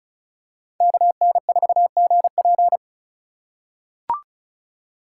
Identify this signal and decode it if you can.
Morse code, heard as a keyed tone, about 34 wpm, 700 Hz. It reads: KN4GP
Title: USA Call Signs - 5 Characters - 3x Character Spacing - 35wpm